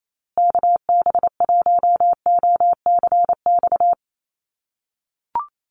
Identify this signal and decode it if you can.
Morse code, heard as a keyed tone, about 28 wpm, 700 Hz. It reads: KB1OCX
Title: USA Call Signs Repeated 3x 28wpm